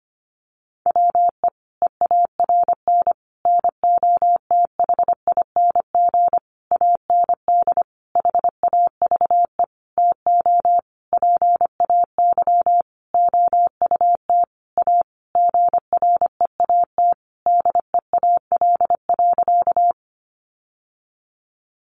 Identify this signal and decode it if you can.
Morse code, heard as a keyed tone, about 25 wpm, 700 Hz. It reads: WE EARN NOTHING AND HAVE TO PAY OUT A GREAT DEAL.
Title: Sentences from Top 1500 Words 25wpm